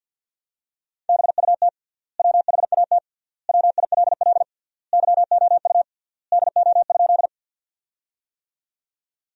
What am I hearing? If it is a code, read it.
Morse code, heard as a keyed tone, about 47 wpm, 700 Hz. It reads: BUT WHAT WILL YOU DO?
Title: 3x Word-Spacing - Sentences from Top 100 Words - 50wpm